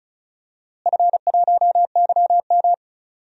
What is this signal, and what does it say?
Morse code, heard as a keyed tone, about 36 wpm, 700 Hz. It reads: F1YM
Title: DXCC-100 Call Signs - 4 Characters - Repeated 3x - 35wpm